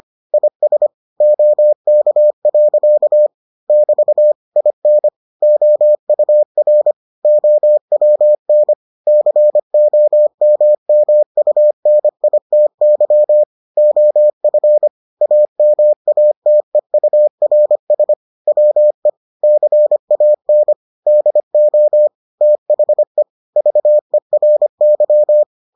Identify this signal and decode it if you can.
Morse code, heard as a keyed tone, about 25 wpm, 600 Hz. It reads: IS OK. <BT> IN OUR OWN COMMUNITY OF AMATEURS WE CAN DO THE VERY